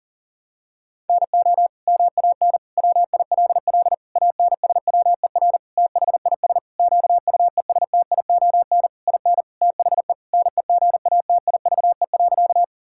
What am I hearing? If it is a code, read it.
Morse code, heard as a keyed tone, about 40 wpm, 700 Hz. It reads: NO MAN WILL ANSWER THIS QUESTION IN THE NEGATIVE.
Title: Sentences from Top 1000 Words - Review Part 6 - 40wpm